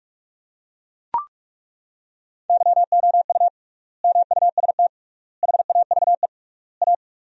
Morse code, heard as a keyed tone, about 42 wpm, 700 Hz. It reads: YOU MUST HAVE A
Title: 3x Word-Spacing - Sentences from Top 500 Words - 45wpm